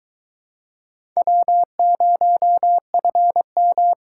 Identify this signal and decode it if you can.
Morse code, heard as a keyed tone, about 23 wpm, 700 Hz. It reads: W0FM